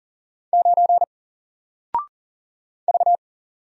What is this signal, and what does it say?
Morse code, heard as a keyed tone, about 40 wpm, 700 Hz. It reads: 9 V
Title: Single Character (Letter:Number:Punctuation:Prosign) Alt 40wpm